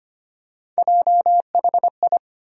Morse code, heard as a keyed tone, about 25 wpm, 700 Hz. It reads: JHI
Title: USA Call Sign Postfixes - 3 Letters 25wpm